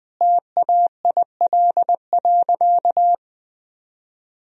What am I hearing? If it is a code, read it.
Morse code, heard as a keyed tone, about 20 wpm, 700 Hz. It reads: TAIL.
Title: Sentences from Top 1000 Words - Review Part 3 - 20wpm